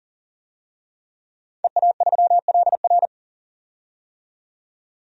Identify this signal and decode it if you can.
Morse code, heard as a keyed tone, about 40 wpm, 700 Hz. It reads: EA3LR
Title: DXCC-100 Call Signs - 5 Characters - 40wpm